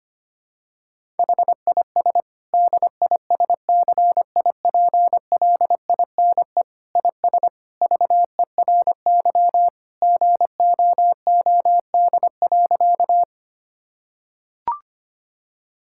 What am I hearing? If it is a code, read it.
Morse code, heard as a keyed tone, about 25 wpm, 700 Hz. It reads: HIS DISCIPLINE IS VERY GOOD.
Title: Sentences from Top 1000 Words - Review Part 6 - 25wpm